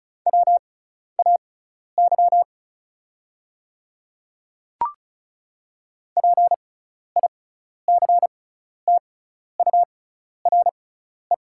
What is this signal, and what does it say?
Morse code, heard as a keyed tone, about 34 wpm, spaced as for 12 wpm, 690 Hz. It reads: WAY PICTURE
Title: 6x Character Spacing - Top 200 Words - 35wpm